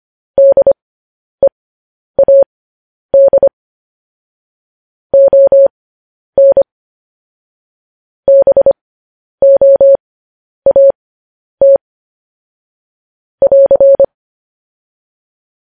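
Morse code, heard as a keyed tone, about 25 wpm, spaced as for 10 wpm, 550 Hz. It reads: DEAD ON BOAT <AR>
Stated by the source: Morse code news headlines broadcast, opening with vvv.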